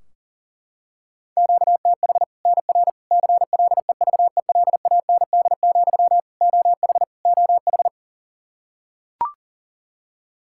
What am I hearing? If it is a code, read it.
Morse code, heard as a keyed tone, about 40 wpm, 700 Hz. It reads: QTH NR CLEVELAND, OH OH
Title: K1USN SST QSO Practice 40wpm